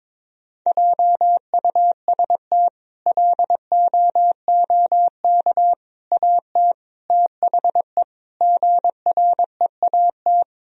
Morse code, heard as a keyed tone, about 22 wpm, 700 Hz. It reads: JUST LOOK AT THE GREAT
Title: Sentences from Top 1000 Words - Review Part 1 - 22wpm